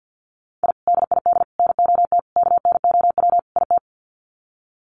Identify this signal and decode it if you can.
Morse code, heard as a keyed tone, about 50 wpm, 700 Hz. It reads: I DID NOT KNOW IT
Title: Sets of 5 Words Encore2 50wpm